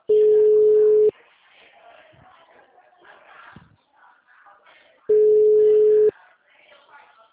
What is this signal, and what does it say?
Ringback tone (Europe, 425 Hz).